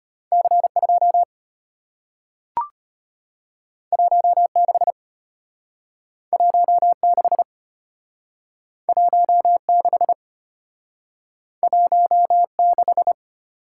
Morse code, heard as a keyed tone, about 33 wpm, 700 Hz. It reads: C 2 1 6 1 6 1 6 1 6